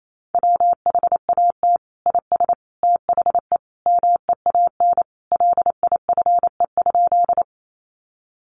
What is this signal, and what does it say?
Morse code, heard as a keyed tone, about 28 wpm, 700 Hz. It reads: WHAT IS THE MEAN LIFE?
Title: Sentences from Top 900 Words 28wpm